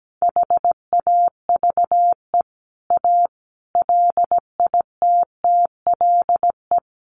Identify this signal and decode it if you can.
Morse code, heard as a keyed tone, about 17 wpm, 700 Hz. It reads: HAVE A LITTLE
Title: Sentences from Top 1300 Words 17wpm